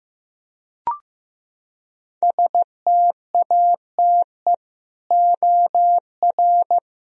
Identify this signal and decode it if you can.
Morse code, heard as a keyed tone, about 15 wpm, 700 Hz. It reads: STATE OR
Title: Sentences from Top 1000 Words - Review Part 3 - 15wpm